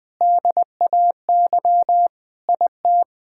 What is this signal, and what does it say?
Morse code, heard as a keyed tone, about 21 wpm, 700 Hz. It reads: DAY IT
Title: Sentences with 4-Word Phrases - 20wpm